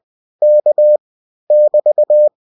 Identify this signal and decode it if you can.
Morse code, heard as a keyed tone, about 20 wpm, 600 Hz. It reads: K <BT>